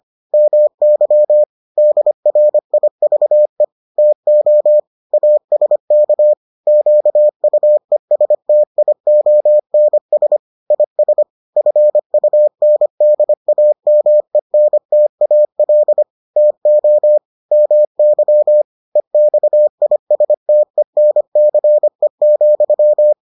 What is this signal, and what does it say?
Morse code, heard as a keyed tone, about 25 wpm, 600 Hz. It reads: MY DRIVE TO ASK QUESTIONS IS FUNDAMENTAL TO MY EXISTENCE,